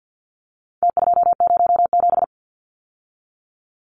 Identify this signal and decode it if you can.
Morse code, heard as a keyed tone, about 50 wpm, 700 Hz. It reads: T207
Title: Numbers - Contest Style - 4-Digit 50wpm